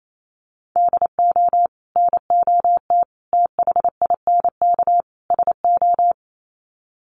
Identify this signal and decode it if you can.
Morse code, heard as a keyed tone, about 28 wpm, 700 Hz. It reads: DO NOT THINK SO
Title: Sets of 4 Words 28wpm